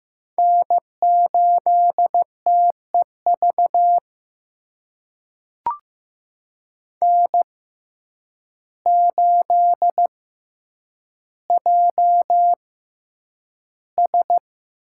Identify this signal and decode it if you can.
Morse code, heard as a keyed tone, about 15 wpm, spaced as for 8 wpm, 700 Hz. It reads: N8TEV N8JS
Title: USA Call Signs - 5 Characters - 6x Character Spacing - 15wpm